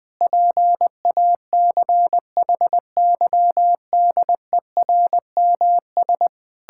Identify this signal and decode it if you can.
Morse code, heard as a keyed tone, about 20 wpm, 700 Hz. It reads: PACHYDERMS
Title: Pangram Challenge v2 - 20wpm